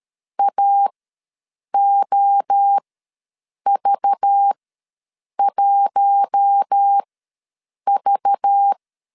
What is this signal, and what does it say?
Morse code, heard as a keyed tone, about 13 wpm, spaced as for 7 wpm, 800 Hz. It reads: AOV1V
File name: LEZIONI / 5 LBVC / 5 LBV ws 12wpm 3015